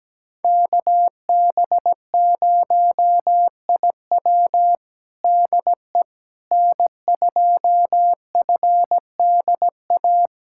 Morse code, heard as a keyed tone, about 17 wpm, 700 Hz. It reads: KB0IW DE N2FDA
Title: USA Callsign DE Callsign 17wpm